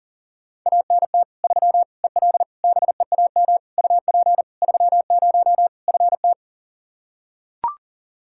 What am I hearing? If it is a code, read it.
Morse code, heard as a keyed tone, about 40 wpm, 700 Hz. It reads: ANT 3 EL BEAM UP 30 FT
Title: Practice QSO - Exchange 2 - Info - 40wpm